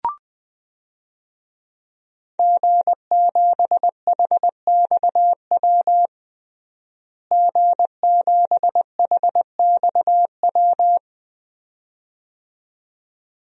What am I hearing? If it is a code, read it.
Morse code, heard as a keyed tone, about 20 wpm, 700 Hz. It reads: G7HXW G7HXW
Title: DXCC-100 Call Signs - 5 Characters - Repeated 2x - 20wpm